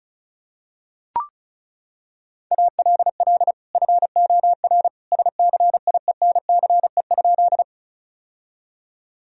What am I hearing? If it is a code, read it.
Morse code, heard as a keyed tone, about 36 wpm, 700 Hz. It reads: ALL FOR SCIENCE?
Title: Sentences from Top 1000 Words - Review Part 2 - 35wpm